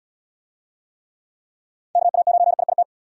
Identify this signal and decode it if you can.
Morse code, heard as a keyed tone, about 76 wpm, 700 Hz. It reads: KA9IIT